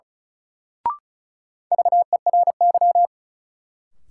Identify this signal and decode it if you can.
Morse code, heard as a keyed tone, about 36 wpm, 700 Hz. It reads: VERY
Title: Sets of 2 Words - New Word Drill - 35wpm